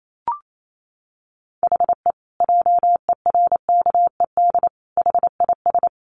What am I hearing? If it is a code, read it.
Morse code, heard as a keyed tone, about 28 wpm, 700 Hz. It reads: HE JERKED HIS
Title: A Sound Of Thunder 28wpm